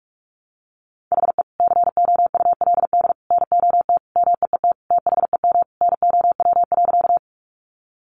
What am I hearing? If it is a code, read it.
Morse code, heard as a keyed tone, about 44 wpm, 700 Hz. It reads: HE COULD NOT MEET THEM NOW.
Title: Sentences from Top 1100 Words 45wpm